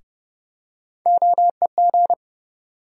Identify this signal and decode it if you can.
Morse code, heard as a keyed tone, about 31 wpm, 700 Hz. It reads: OEG
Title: USA Call Sign Postfixes - 3 Letters 30wpm